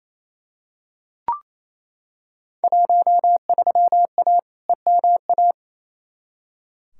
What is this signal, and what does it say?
Morse code, heard as a keyed tone, about 28 wpm, 700 Hz. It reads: 13A EMA
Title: Field Day Exchanges Encore - 28wpm